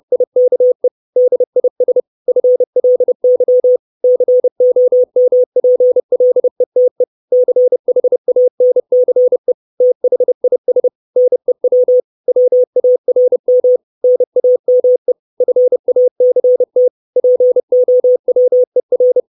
Morse code, heard as a keyed tone, about 30 wpm, 500 Hz. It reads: IKE DIS FLY COMPLETE CHANCE THIS NEW WARM NAME FACT POWER
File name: 632 Words 30WPM